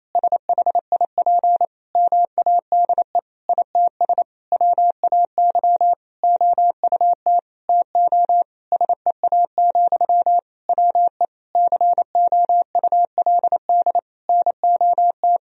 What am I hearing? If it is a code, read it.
Morse code, heard as a keyed tone, about 27 wpm, 700 Hz. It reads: SHIP MADE ITS WAY OUT TO SEA, WE COULD NOT
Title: Sentences from Top 1000 Words - Review Part 2 - 28wpm